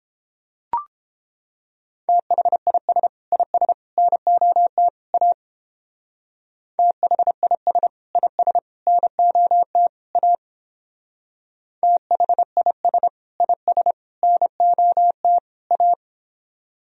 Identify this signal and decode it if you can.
Morse code, heard as a keyed tone, about 31 wpm, 700 Hz. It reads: THIS IS NOT A THIS IS NOT A THIS IS NOT A